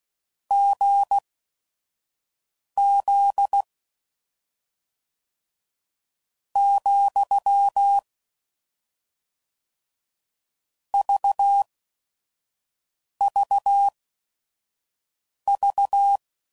Morse code, heard as a keyed tone, about 16 wpm, spaced as for 5 wpm, 800 Hz. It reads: GZ , VVV
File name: Combinations / Difficult Combos / GZ GZ7